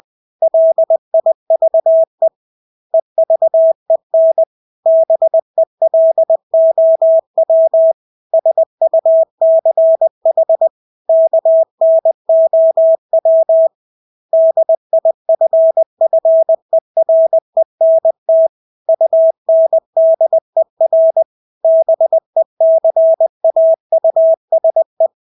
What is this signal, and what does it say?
Morse code, heard as a keyed tone, about 20 wpm, 650 Hz. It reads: LIVE EVEN BELOW SUCH KNOW DIFFERENT UNDER BECAUSE
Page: Engelske ord 20 wpm